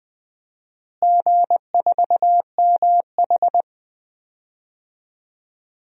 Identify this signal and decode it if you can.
Morse code, heard as a keyed tone, about 20 wpm, 700 Hz. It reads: G4MH
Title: DXCC-100 Call Signs - 4 Characters - Repeated 3x - 20wpm